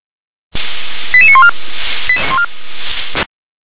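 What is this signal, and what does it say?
Five-tone selective call (ZVEI), heard as quick runs of five tones, tones of about 70 ms. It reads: 88914
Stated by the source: ZVEI five-tone selective call received by SDR.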